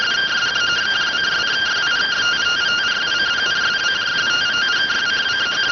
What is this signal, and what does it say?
Two-tone teleprinter signal, about 170 Hz shift.